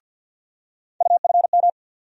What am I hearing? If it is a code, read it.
Morse code, heard as a keyed tone, about 51 wpm, 700 Hz. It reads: UFM
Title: USA Call Sign Postfixes - 3 Letters 50wpm